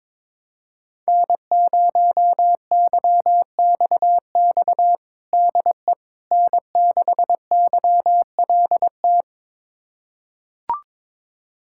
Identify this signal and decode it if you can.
Morse code, heard as a keyed tone, about 22 wpm, 700 Hz. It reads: N0YXX DE N6YLT
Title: USA Callsign DE Callsign - 5 Characters - 22wpm